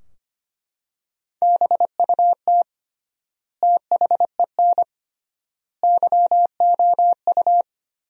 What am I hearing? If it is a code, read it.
Morse code, heard as a keyed tone, about 25 wpm, 700 Hz. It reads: BUT THEN YOU
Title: 3x Word-Spacing - Sentences from Top 300 Words - 25wpm